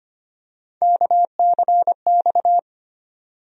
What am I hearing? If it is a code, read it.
Morse code, heard as a keyed tone, about 25 wpm, 700 Hz. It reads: KCX
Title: USA Call Sign Postfixes - 3 Letters 25wpm